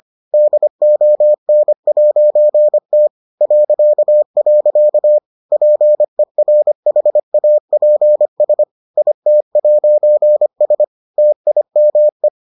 Morse code, heard as a keyed tone, about 25 wpm, 600 Hz. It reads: DON'T .. PERHAPS IT'S TIME